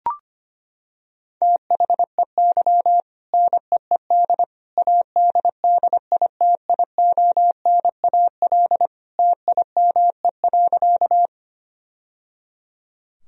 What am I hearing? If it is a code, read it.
Morse code, heard as a keyed tone, about 25 wpm, 700 Hz. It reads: THEY NEED ADDITIONAL TIME.
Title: Sentences from Top 1000 Words - Review Part 4 - 25wpm